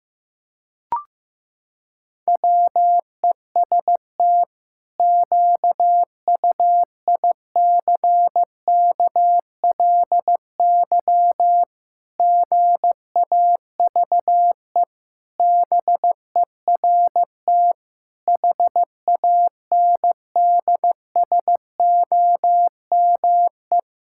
Morse code, heard as a keyed tone, about 16 wpm, 700 Hz. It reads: WEST QUICKLY GAVE BERT HANDSOME
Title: Pangram Challenge v2 - 15wpm